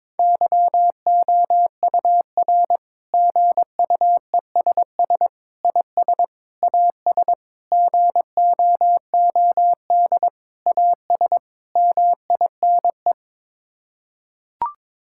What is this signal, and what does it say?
Morse code, heard as a keyed tone, about 22 wpm, 700 Hz. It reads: YOUR GUESS IS AS GOOD AS MINE
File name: practice / 22wpm / Sets of 7 Words - 22wpm